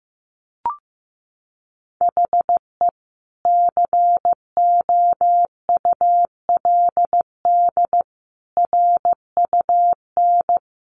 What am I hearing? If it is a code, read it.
Morse code, heard as a keyed tone, about 15 wpm, 700 Hz. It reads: HE COULD RUN
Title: Sentences from Top 300 Words 15wpm